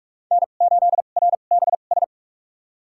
Morse code, heard as a keyed tone, about 46 wpm, spaced as for 33 wpm, 690 Hz. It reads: N8RBS
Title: USA Call Signs - 5 Characters - 2x Character Spacing - 45wpm